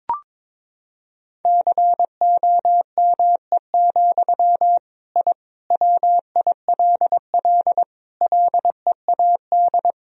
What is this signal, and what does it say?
Morse code, heard as a keyed tone, about 22 wpm, 700 Hz. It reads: COME, I WILL LEAD
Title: Sentences from Top 1000 Words - Review Part 2 - 22wpm